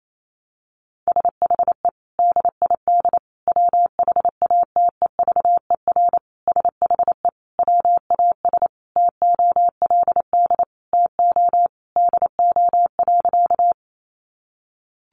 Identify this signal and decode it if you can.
Morse code, heard as a keyed tone, about 28 wpm, 700 Hz. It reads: SHE DID WHATEVER SHE WAS TOLD TO DO.
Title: Sentences from Top 1100 Words 28wpm